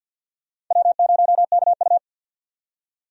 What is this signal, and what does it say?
Morse code, heard as a keyed tone, about 53 wpm, 700 Hz. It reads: W0KU
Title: USA Call Signs - 4 Characters - 50wpm